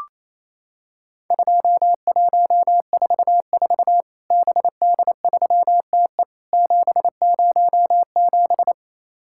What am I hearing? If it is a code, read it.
Morse code, heard as a keyed tone, about 28 wpm, 700 Hz. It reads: 2144 BD3TE 707
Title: Worked All Europe DX Contest - QTC Exchange - 28wpm